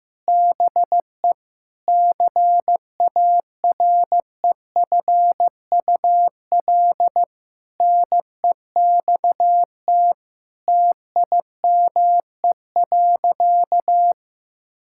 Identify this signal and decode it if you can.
Morse code, heard as a keyed tone, about 14 wpm, 700 Hz. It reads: BE CAREFUL NEXT TIME.
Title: Sentences from Top 1600 Words 15wpm